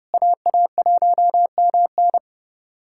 Morse code, heard as a keyed tone, about 32 wpm, 700 Hz. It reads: AA1MN
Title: USA Callsign DE Callsign - 5 Characters - 30wpm